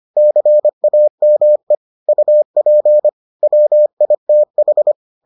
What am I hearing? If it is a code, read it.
Morse code, heard as a keyed tone, about 25 wpm, 600 Hz. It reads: CAME UP WITH